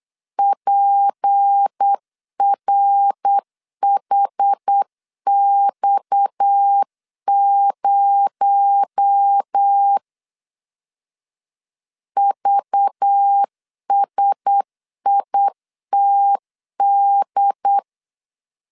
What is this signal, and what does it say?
Morse code, heard as a keyed tone, about 8 wpm, 800 Hz. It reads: PRHX0 VSITD
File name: CW / DET / 1 6 1500C S C 8wpm 5219